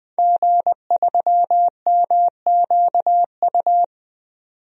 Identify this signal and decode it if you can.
Morse code, heard as a keyed tone, about 21 wpm, 700 Hz. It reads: G3MQU
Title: DXCC-100 Call Signs - 5 Characters - 20wpm